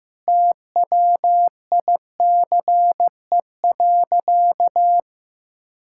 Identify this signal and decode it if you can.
Morse code, heard as a keyed tone, about 15 wpm, 700 Hz. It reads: TWICE.
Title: Sentences from Top 1600 Words 15wpm